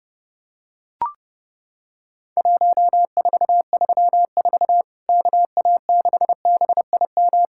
Morse code, heard as a keyed tone, about 30 wpm, 700 Hz. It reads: 1434 KA6BIM
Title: Worked All Europe DX Contest - QTC Exchange - 30wpm